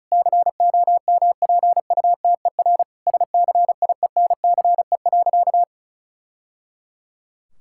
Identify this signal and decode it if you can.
Morse code, heard as a keyed tone, about 36 wpm, 700 Hz. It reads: COMPUTER SCIENCE.
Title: Sentences from Top 1000 Words - Review Part 6 - 35wpm